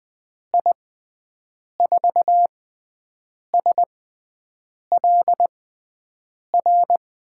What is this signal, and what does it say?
Morse code, heard as a keyed tone, about 20 wpm, spaced as for 7 wpm, 700 Hz. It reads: I4SLR
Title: USA Call Signs - 6 Characters - 6x Character Spacing - 20wpm